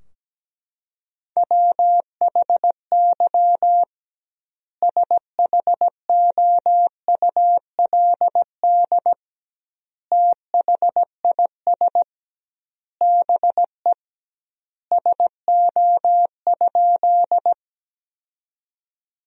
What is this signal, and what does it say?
Morse code, heard as a keyed tone, about 17 wpm, 700 Hz. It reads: WHY SHOULD THIS BE SO?
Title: 2x Word-Spacing - Sentences from Top 300 Words - 17wpm